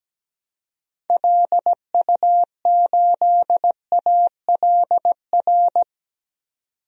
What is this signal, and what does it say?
Morse code, heard as a keyed tone, about 17 wpm, 700 Hz. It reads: LU8ALR
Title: DXCC-100 Call Signs - 6 Characters - 17wpm